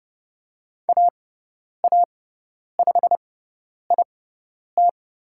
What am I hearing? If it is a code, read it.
Morse code, heard as a keyed tone, about 30 wpm, spaced as for 10 wpm, 700 Hz. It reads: AA5IT
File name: Lesson 18g - Letter m - Call Signs 30wpm